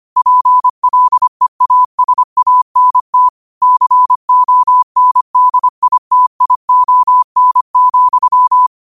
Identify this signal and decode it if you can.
Morse code, heard as a keyed tone, about 25 wpm, 1000 Hz. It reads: PLEASANT CONDITION,
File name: qotdfiles / 25wpm